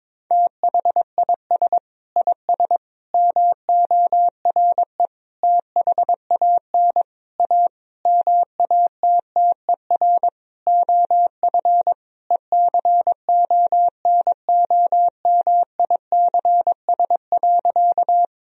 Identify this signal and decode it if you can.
Morse code, heard as a keyed tone, about 22 wpm, 700 Hz. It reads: THIS IS MORE THAN A MATTER OF ECONOMICS.
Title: Sentences from Top 1000 Words - Review Part 3 - 22wpm